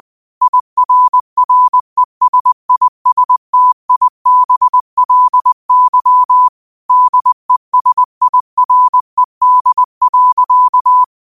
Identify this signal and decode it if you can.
Morse code, heard as a keyed tone, about 20 wpm, 1000 Hz. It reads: IRRESISTIBLY DESIRED.
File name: qotdfiles / 20wpm